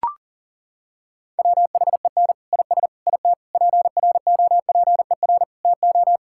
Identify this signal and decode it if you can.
Morse code, heard as a keyed tone, about 40 wpm, 700 Hz. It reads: WHEN IS IT PROPER TO
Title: Sentences from Top 1000 Words - Review Part 5 - 40wpm